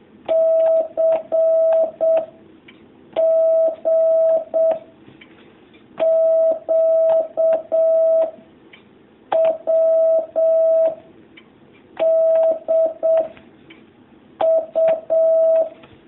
Morse code, heard as a keyed tone, about 7 wpm, 650 Hz. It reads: C G Q W D U
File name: jh8cbh / 2019cw